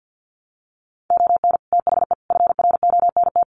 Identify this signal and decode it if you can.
Morse code, heard as a keyed tone, about 50 wpm, 700 Hz. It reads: ON THE FRONT
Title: Sets of 5 Words Encore 50wpm